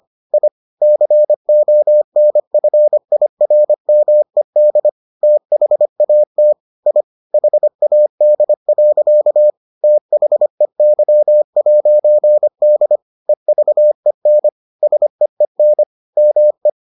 Morse code, heard as a keyed tone, about 25 wpm, 600 Hz. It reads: I CONFIRMED THAT I HAD. THEY'D EVEN SEEN ME